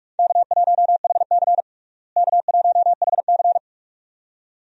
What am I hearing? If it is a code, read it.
Morse code, heard as a keyed tone, about 46 wpm, 700 Hz. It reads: K1HC K1HC
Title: USA Call Signs - 4 Characters - Repeated 2x - 45wpm